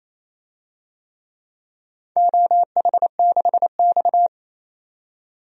Morse code, heard as a keyed tone, about 28 wpm, 700 Hz. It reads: OH6X
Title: DXCC-100 Call Signs - 4 Characters - 28wpm